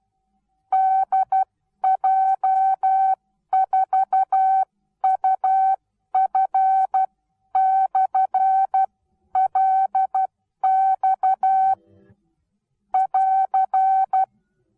Morse code, heard as a keyed tone, about 12 wpm, spaced as for 10 wpm, 760 Hz. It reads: DJ4UF/LX <AR>
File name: Onlinelehrgang / morsen / DJ4UF Morsen19